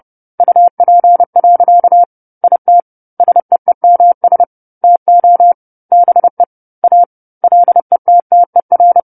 Morse code, heard as a keyed tone, about 30 wpm, 700 Hz. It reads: UP. IT SEEMS TO BE A LETTER
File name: 10-Chapter-10 30 WPM